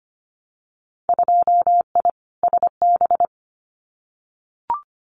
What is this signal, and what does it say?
Morse code, heard as a keyed tone, about 25 wpm, 700 Hz. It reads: 2I SB
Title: Winter Field Day Exchanges Encore 25wpm